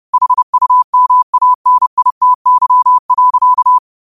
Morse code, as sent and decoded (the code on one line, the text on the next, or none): .... ..- -- .- -. .. - -.-- .-.-.-
HUMANITY.